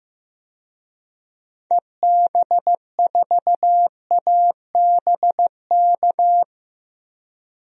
Morse code, heard as a keyed tone, about 16 wpm, 700 Hz. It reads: EB4ABK
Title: DXCC-100 Call Signs - 6 Characters - Repeated 3x - 15wpm